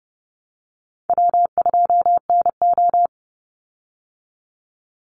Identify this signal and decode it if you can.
Morse code, heard as a keyed tone, about 30 wpm, 700 Hz. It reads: W2NO
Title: Call Signs Repeated 3x - 30wpm